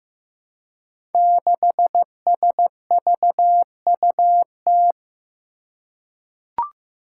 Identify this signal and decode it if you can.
Morse code, heard as a keyed tone, about 15 wpm, 700 Hz. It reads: 6SVUT
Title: Five Challenging Letter-Number Combinations 15wpm